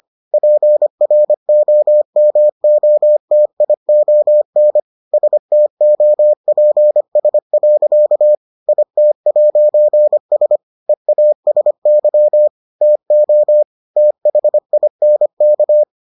Morse code, heard as a keyed tone, about 25 wpm, 600 Hz. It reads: PROMOTION STOPS. IT'S EASY TO THINK